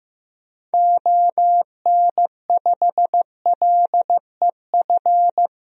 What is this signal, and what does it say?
Morse code, heard as a keyed tone, about 15 wpm, 700 Hz. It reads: ON5LEF
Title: DXCC-100 Call Signs - 6 Characters - 15wpm